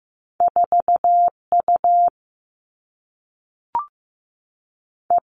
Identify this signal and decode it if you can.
Morse code, heard as a keyed tone, about 15 wpm, 700 Hz. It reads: 4 U E
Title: All Cut Numbers - 2-Digit 15wpm